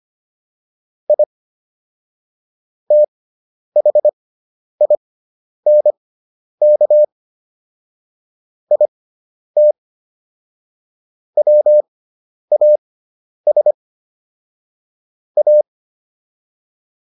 Morse code, heard as a keyed tone, about 25 wpm, spaced as for 10 wpm, 600 Hz. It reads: I THINK IT WAS A